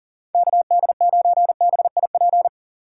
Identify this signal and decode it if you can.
Morse code, heard as a keyed tone, about 40 wpm, 700 Hz. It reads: KD9BIP
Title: USA Call Signs - 6 Characters - 6x Character Spacing - 40wpm